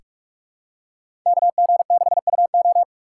Morse code, heard as a keyed tone, about 46 wpm, 700 Hz. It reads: KG6UO
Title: USA Call Signs - 2x Character Spacing - 45wpm